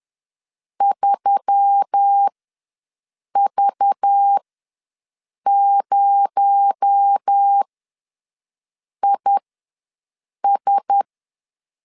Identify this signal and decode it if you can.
Morse code, heard as a keyed tone, about 10 wpm, spaced as for 5 wpm, 800 Hz. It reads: 3V0IS
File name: LEZIONI / 6 PKX0 / 6 PKX0 ws 10wpm 4220